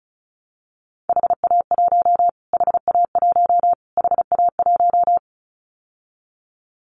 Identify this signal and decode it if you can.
Morse code, heard as a keyed tone, about 34 wpm, 700 Hz. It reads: HA1 HA1 HA1
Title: Callsign Prefixes for Top 25 DXCC - Repeated 3x - 35wpm